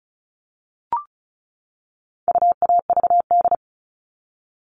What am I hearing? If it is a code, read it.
Morse code, heard as a keyed tone, about 34 wpm, 700 Hz. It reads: UAVD
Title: All Cut Numbers - 4-Digit 35wpm